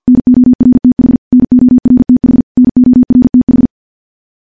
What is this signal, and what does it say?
Morse code, heard as a keyed tone, about 51 wpm, 250 Hz. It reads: NORTH NORTH NORTH
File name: morse / morse-training-50WPM